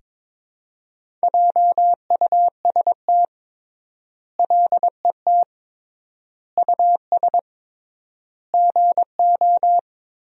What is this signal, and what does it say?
Morse code, heard as a keyed tone, about 21 wpm, 700 Hz. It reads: JUST LET US GO
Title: 3x Word-Spacing - Sentences from Top 500 Words - 22wpm